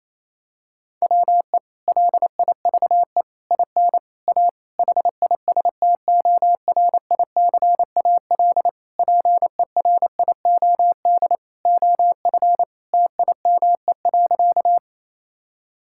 Morse code, heard as a keyed tone, about 27 wpm, 700 Hz. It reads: WE LIVE IN A HISTORICAL PERIOD OF TIME.
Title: Sentences from Top 1000 Words - Review Part 5 - 28wpm